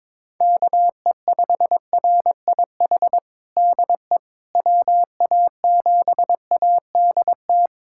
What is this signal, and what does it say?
Morse code, heard as a keyed tone, about 22 wpm, 700 Hz. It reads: KE5RIH DE WA7ADT
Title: USA Callsign DE Callsign - 6 Characters - 22wpm